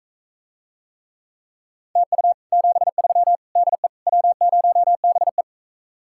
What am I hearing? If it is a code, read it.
Morse code, heard as a keyed tone, about 45 wpm, 700 Hz. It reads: TU 73 DE W0BE